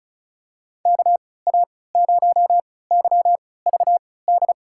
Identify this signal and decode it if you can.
Morse code, heard as a keyed tone, about 36 wpm, spaced as for 20 wpm, 700 Hz. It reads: KA0YVD
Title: USA Call Signs - 6 Characters - 3x Character Spacing - 35wpm